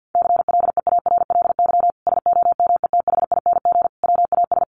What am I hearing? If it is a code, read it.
Morse code, heard as a keyed tone, about 50 wpm, 700 Hz. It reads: CLEARLY SOMETHING WAS
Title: Sentences from Top 1300 Words 50wpm